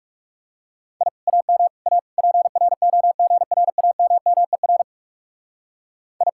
Morse code, heard as a keyed tone, about 36 wpm, 700 Hz. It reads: I AM A PROGRAMMER I
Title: Speed-Racing Common QSO Phrases - 30wpm